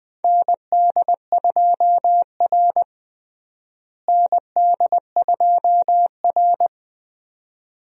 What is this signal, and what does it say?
Morse code, heard as a keyed tone, about 20 wpm, 700 Hz. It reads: ND2R ND2R
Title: USA Call Signs - 4 Characters - Repeated 2x - 20wpm